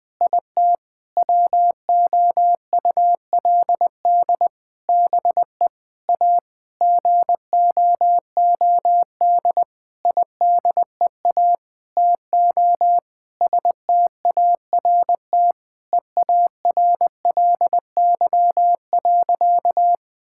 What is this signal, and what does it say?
Morse code, heard as a keyed tone, about 21 wpm, 700 Hz. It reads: IT WOULD BE A GOOD IDEA TO START EARLY.
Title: Sentences with 9-Word Phrases - 20wpm